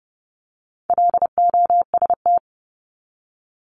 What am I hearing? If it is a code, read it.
Morse code, heard as a keyed tone, about 29 wpm, 700 Hz. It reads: LOST
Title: Top 1251-1500 Words 30wpm